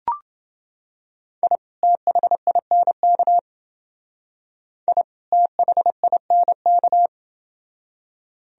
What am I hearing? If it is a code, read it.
Morse code, heard as a keyed tone, about 28 wpm, 700 Hz. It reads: I THINK I THINK